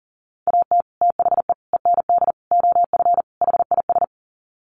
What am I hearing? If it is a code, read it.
Morse code, heard as a keyed tone, about 40 wpm, 700 Hz. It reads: AT THE END OF HIS
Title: Sets of 5 Words Encore 40wpm